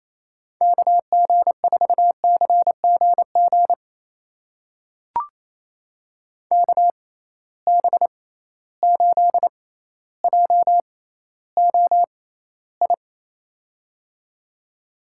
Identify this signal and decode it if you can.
Morse code, heard as a keyed tone, about 28 wpm, spaced as for 14 wpm, 700 Hz. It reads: KG4CGG KB8JOI
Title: USA Call Signs - 6x Character Spacing - 28wpm